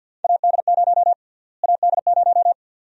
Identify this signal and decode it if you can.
Morse code, heard as a keyed tone, about 51 wpm, 700 Hz. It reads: AD0 AD0
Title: USA Call Sign Prefixes - Repeated 2x - 50wpm